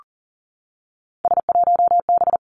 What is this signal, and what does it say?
Morse code, heard as a keyed tone, about 40 wpm, 700 Hz. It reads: S1B
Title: Three Challenging Letter-Number Combinations 40wpm